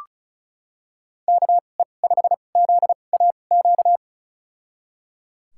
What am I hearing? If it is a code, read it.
Morse code, heard as a keyed tone, about 36 wpm, spaced as for 25 wpm, 700 Hz. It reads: KE5ZAQ
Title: USA Call Signs - 2x Character Spacing - 35wpm